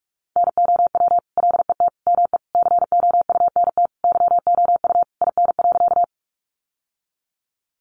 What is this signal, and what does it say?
Morse code, heard as a keyed tone, about 44 wpm, 700 Hz. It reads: NOW LET ME COUNT YOU IN.
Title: Sentences from Top 1500 Words 45wpm